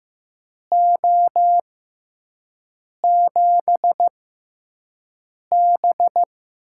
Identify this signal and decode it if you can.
Morse code, heard as a keyed tone, about 15 wpm, spaced as for 5 wpm, 700 Hz. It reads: O7B
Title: USA Call Signs - 6x Character Spacing - 15wpm